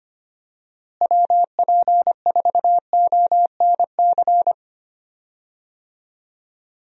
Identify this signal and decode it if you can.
Morse code, heard as a keyed tone, about 25 wpm, 700 Hz. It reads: WP4ONC
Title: USA Call Signs - 6 Characters - Repeated 3x - 25wpm